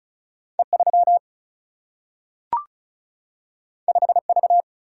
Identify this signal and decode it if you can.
Morse code, heard as a keyed tone, about 36 wpm, 700 Hz. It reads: E3 5V
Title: Single-Letter Single-Number 35wpm